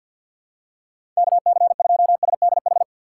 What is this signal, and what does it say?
Morse code, heard as a keyed tone, about 46 wpm, 700 Hz. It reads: KC2SDH
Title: USA Call Signs - 6 Characters - 3x Character Spacing - 50wpm